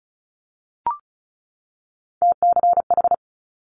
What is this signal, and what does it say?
Morse code, heard as a keyed tone, about 36 wpm, 700 Hz. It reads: TCH
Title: Multi-letter Phonemes 35wpm